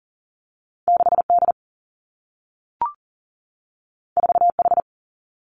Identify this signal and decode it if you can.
Morse code, heard as a keyed tone, about 40 wpm, 700 Hz. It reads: 6D 4H
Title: Two Challenging Letter-Number Combinations 40wpm